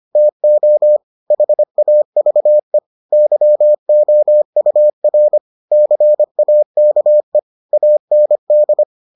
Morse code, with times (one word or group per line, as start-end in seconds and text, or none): 0.2-1.0 TO
1.3-2.8 HAVE
3.1-5.4 YOUR
5.7-7.4 CAKE
7.7-8.8 AND